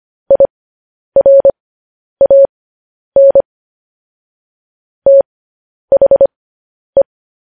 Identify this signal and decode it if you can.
Morse code, heard as a keyed tone, about 25 wpm, spaced as for 10 wpm, 550 Hz. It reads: IRAN THE